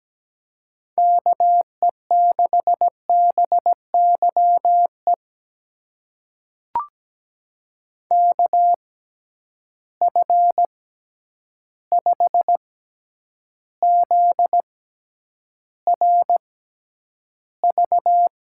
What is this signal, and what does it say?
Morse code, heard as a keyed tone, about 17 wpm, spaced as for 9 wpm, 700 Hz. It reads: KE6BYE KF5ZRV
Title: USA Call Signs - 6 Characters - 6x Character Spacing - 17wpm